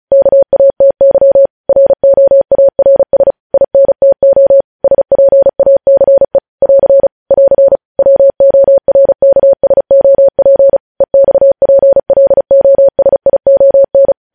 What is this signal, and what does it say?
Morse code, heard as a keyed tone, about 36 wpm, 550 Hz. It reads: KATY ROARS INTO SPACE <AR> <AR> WORKSOP EXPLOSION
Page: news headlines in morse code at 35 wpm